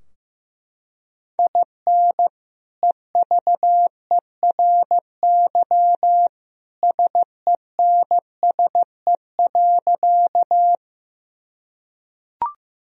Morse code, heard as a keyed tone, about 16 wpm, 700 Hz. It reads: IN EVERY SENSE.
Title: Sentences from Top 1000 Words - Review Part 4 - 15wpm